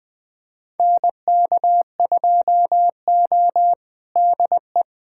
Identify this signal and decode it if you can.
Morse code, heard as a keyed tone, about 20 wpm, 700 Hz. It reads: NK2O DE
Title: USA Callsign DE Callsign - 4 Characters - 20wpm